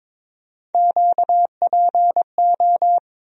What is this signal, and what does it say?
Morse code, heard as a keyed tone, about 23 wpm, 700 Hz. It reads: QPO
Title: USA Call Sign Postfixes - 3 Letters 22wpm